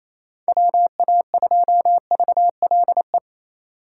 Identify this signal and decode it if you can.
Morse code, heard as a keyed tone, about 33 wpm, 700 Hz. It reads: WA2VLE
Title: Speed Racing - USA 2x3 Call Signs 28wpm